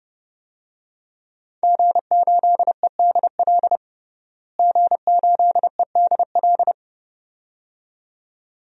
Morse code, heard as a keyed tone, about 30 wpm, 700 Hz. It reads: G8EDL G8EDL
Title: DXCC-100 Call Signs - 5 Characters - Repeated 2x - 30wpm